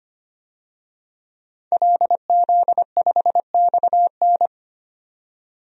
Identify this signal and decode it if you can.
Morse code, heard as a keyed tone, about 25 wpm, 700 Hz. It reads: LZ5XN
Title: DXCC-100 Call Signs - 5 Characters - 25wpm